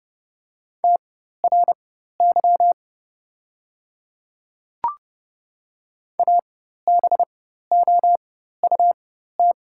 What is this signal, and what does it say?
Morse code, heard as a keyed tone, about 30 wpm, spaced as for 14 wpm, 700 Hz. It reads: TRY ABOUT
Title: 4x Character Spacing - Top 200 Words - 30wpm